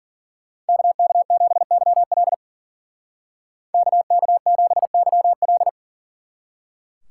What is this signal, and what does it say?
Morse code, heard as a keyed tone, about 50 wpm, 700 Hz. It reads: KK7YL KK7YL
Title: Speed Racing - USA 2x2 Call Signs 40wpm